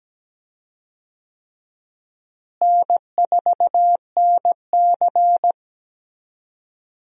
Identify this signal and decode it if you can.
Morse code, heard as a keyed tone, about 17 wpm, 700 Hz. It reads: N4NC
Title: USA Call Signs - 4 Characters - Repeated 3x - 17wpm